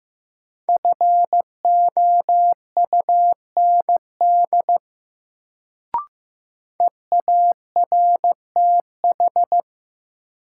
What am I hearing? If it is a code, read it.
Morse code, heard as a keyed tone, about 15 wpm, 700 Hz. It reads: FOUND EARTH
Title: Sets of 5 Words Encore - New Word Drill - 15wpm